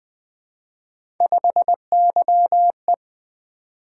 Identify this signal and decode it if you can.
Morse code, heard as a keyed tone, about 20 wpm, 700 Hz. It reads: 5YE
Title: Three Letter-Number Combinations - 20wpm